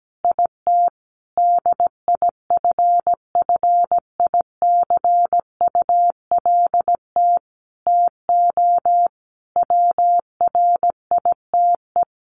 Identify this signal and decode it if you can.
Morse code, heard as a keyed tone, about 17 wpm, 700 Hz. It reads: IT DIFFICULT TO WRITE
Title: Sentences from Top 1500 Words 17wpm